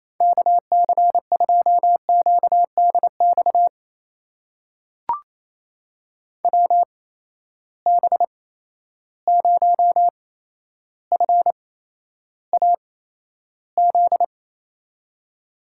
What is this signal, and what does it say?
Morse code, heard as a keyed tone, about 28 wpm, spaced as for 12 wpm, 700 Hz. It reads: KC2QDX WB0FAZ
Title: USA Call Signs - 6 Characters - 8x Character Spacing - 28wpm